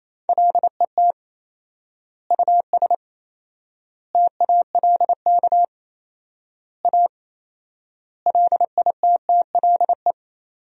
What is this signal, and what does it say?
Morse code, heard as a keyed tone, about 28 wpm, 700 Hz. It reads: LET US TALK A LITTLE
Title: 4x Word-Spacing - Sentences from Top 400 Words - 28wpm